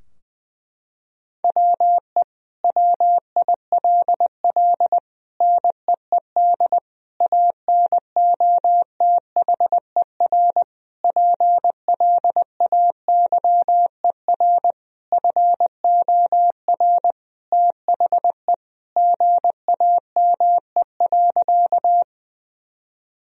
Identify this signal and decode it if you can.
Morse code, heard as a keyed tone, about 20 wpm, 700 Hz. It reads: WE WILL NEED ANOTHER PLAYER FOR THE GAME.
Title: Sentences from Top 1000 Words - Review Part 4 - 20wpm